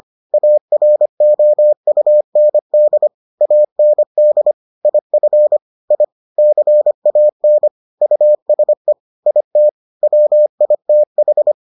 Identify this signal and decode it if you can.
Morse code, heard as a keyed tone, about 25 wpm, 600 Hz. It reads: AROUND AND IF I CAN USE IT WITH